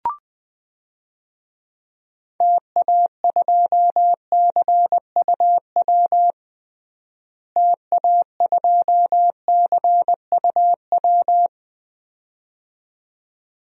Morse code, heard as a keyed tone, about 21 wpm, 700 Hz. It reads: TA2CUW TA2CUW
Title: DXCC-100 Call Signs - 6 Characters - Repeated 2x - 20wpm